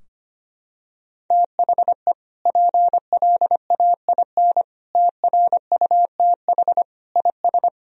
Morse code, heard as a keyed tone, about 25 wpm, 700 Hz. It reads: THE PLAIN TRUTH IS
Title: Sentences from Top 1000 Words - Review Part 2 - 25wpm